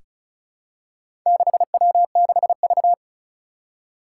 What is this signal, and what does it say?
Morse code, heard as a keyed tone, about 36 wpm, 700 Hz. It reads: 6W6V
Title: Four Challenging Letter-Number Combinations 35wpm